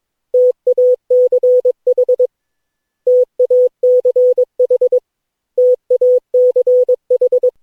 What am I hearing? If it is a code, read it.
Morse code, heard as a keyed tone, about 22 wpm, 500 Hz. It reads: TACH TACH TACH
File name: cw / Session9 cw training